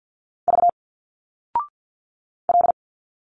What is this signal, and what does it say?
Morse code, heard as a keyed tone, about 50 wpm, 700 Hz. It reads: V L
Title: Single Letters 50wpm